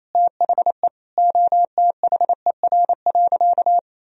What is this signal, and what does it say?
Morse code, heard as a keyed tone, about 27 wpm, 700 Hz. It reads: THE OTHER.
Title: Sentences from Top 1000 Words - Review Part 2 - 28wpm